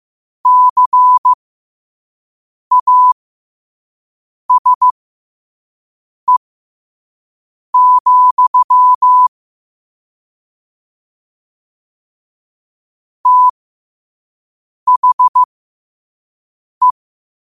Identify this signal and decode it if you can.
Morse code, heard as a keyed tone, about 15 wpm, spaced as for 5 wpm, 1000 Hz. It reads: CASE, THE
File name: qotdfiles / 5wpm